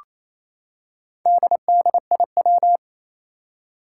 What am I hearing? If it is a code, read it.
Morse code, heard as a keyed tone, about 28 wpm, 700 Hz. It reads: DDIW
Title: Four Challenging Letter-Number Combinations 28wpm